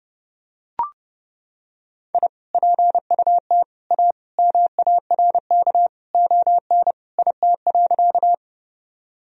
Morse code, heard as a keyed tone, about 30 wpm, 700 Hz. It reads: I PUT A MARK ON IT.
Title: Sentences from Top 1000 Words - Review Part 1 - 30wpm